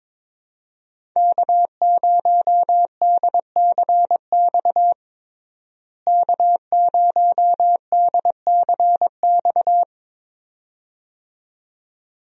Morse code, heard as a keyed tone, about 22 wpm, 700 Hz. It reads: K0DCX K0DCX
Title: USA Call Signs - 5 Characters - Repeated 2x - 22wpm